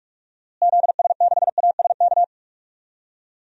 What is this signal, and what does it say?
Morse code, heard as a keyed tone, about 44 wpm, 700 Hz. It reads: ZS6ASK
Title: DXCC-100 Call Signs - 6 Characters - Repeated 3x - 45wpm